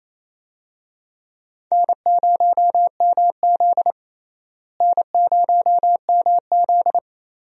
Morse code, heard as a keyed tone, about 28 wpm, 700 Hz. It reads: N0MZ N0MZ
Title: USA Call Signs - 4 Characters - Repeated 2x - 28wpm